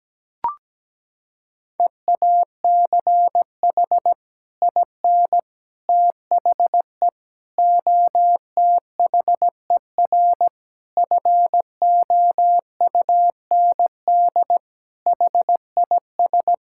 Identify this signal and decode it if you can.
Morse code, heard as a keyed tone, about 17 wpm, 700 Hz. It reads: EACH IN THE OTHER FOUND HIS
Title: Sentences from Top 1600 Words 17wpm